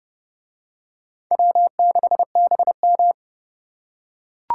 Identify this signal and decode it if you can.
Morse code, heard as a keyed tone, about 30 wpm, 700 Hz. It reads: W6BM